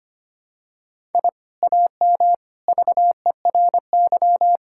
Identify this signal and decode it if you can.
Morse code, heard as a keyed tone, about 25 wpm, 700 Hz. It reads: I AM VERY
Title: Sentences from Top 1000 Words - Review Part 6 - 25wpm